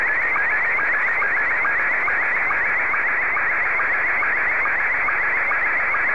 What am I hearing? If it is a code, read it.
SSTV (slow-scan television), Scottie 1.